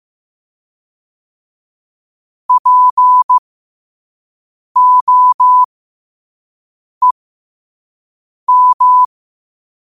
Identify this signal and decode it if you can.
Morse code, heard as a keyed tone, about 15 wpm, spaced as for 5 wpm, 1000 Hz. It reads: POEM
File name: qotdfiles / 5wpm